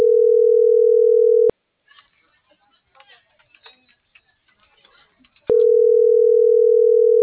Ringback tone (North America), 440 + 480 Hz.